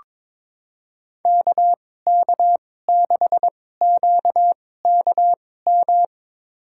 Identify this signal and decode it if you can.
Morse code, heard as a keyed tone, about 22 wpm, spaced as for 16 wpm, 700 Hz. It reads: KK6QKM
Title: USA Call Signs - 6 Characters - 2x Character Spacing - 22wpm